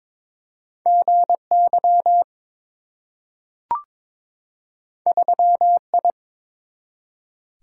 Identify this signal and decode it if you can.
Morse code, heard as a keyed tone, about 23 wpm, 700 Hz. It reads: GY 3I
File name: practice / 22wpm / Two Letter-Number Combinations - 22wpm